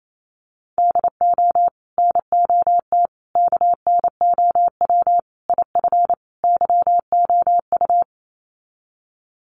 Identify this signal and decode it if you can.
Morse code, heard as a keyed tone, about 28 wpm, 700 Hz. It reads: DO NOT KNOW IF YOU
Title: Sets of 5 Words 28wpm